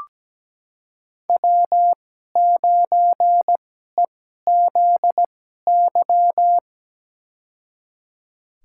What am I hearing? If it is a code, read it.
Morse code, heard as a keyed tone, about 17 wpm, spaced as for 12 wpm, 700 Hz. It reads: W9EZY
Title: USA Call Signs - 5 Characters - 2x Character Spacing - 17wpm